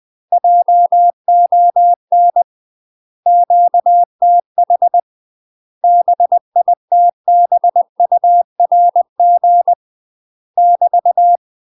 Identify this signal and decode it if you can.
Morse code, heard as a keyed tone, about 20 wpm, 700 Hz. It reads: JON QTH BITBURG <BT>
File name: UK MASTER QSOs TRAINING / 20WPM / UK MASTER QSOs 0008